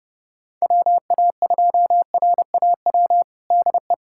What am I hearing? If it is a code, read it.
Morse code, heard as a keyed tone, about 32 wpm, 700 Hz. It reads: WA2RAW DE
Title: USA Callsign DE Callsign - 6 Characters - 30wpm